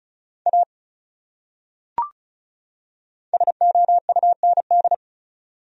Morse code, heard as a keyed tone, about 36 wpm, 700 Hz. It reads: A SOUND
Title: Top 100 Words 35wpm